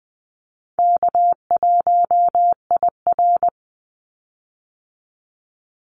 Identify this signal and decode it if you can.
Morse code, heard as a keyed tone, about 20 wpm, 700 Hz. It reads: K1IR
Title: Easy Call Signs Repeated 3x - 20wpm